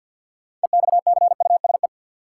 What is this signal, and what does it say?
Morse code, heard as a keyed tone, about 48 wpm, 700 Hz. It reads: EXCUSE
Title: Sentences from Top 700 Words - New Word Drill - 50wpm